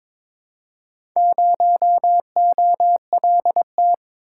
Morse code, heard as a keyed tone, about 21 wpm, 690 Hz. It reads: 0OLT
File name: practice / 22wpm / Four Letter-Number Combinations - 22wpm